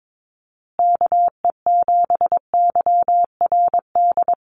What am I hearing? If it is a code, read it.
Morse code, heard as a keyed tone, about 22 wpm, 700 Hz. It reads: KE7YRD
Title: USA Call Signs Repeated 2x 22wpm